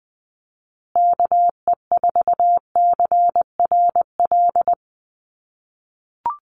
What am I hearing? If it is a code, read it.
Morse code, heard as a keyed tone, about 20 wpm, 700 Hz. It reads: KE4CRL
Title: USA Call Signs Repeated 3x 20wpm